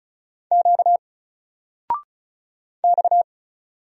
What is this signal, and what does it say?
Morse code, heard as a keyed tone, about 35 wpm, 700 Hz. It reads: Q X
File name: Single-Letter-Number-35wpm